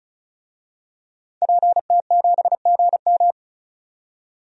Morse code, heard as a keyed tone, about 36 wpm, 700 Hz. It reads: PT7GM
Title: DXCC-100 Call Signs - 5 Characters - 35wpm